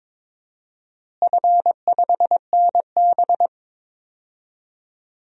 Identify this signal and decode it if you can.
Morse code, heard as a keyed tone, about 22 wpm, 700 Hz. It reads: F5NB
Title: DXCC-100 Call Signs - 4 Characters - Repeated 2x - 22wpm